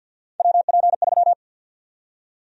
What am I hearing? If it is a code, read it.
Morse code, heard as a keyed tone, about 51 wpm, 700 Hz. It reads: WP3
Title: USA Call Sign Prefixes - 50wpm